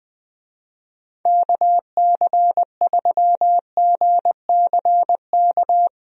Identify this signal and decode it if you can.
Morse code, heard as a keyed tone, about 21 wpm, 700 Hz. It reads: KC3GCK
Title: USA Call Signs - 6 Characters - Repeated 3x - 20wpm